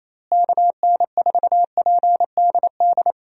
Morse code, heard as a keyed tone, about 27 wpm, 700 Hz. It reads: KN4PDD
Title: USA Call Signs - 6 Characters - Repeated 2x - 28wpm